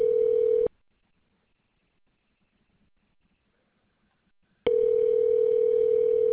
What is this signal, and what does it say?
Ringback tone (North America), 440 + 480 Hz.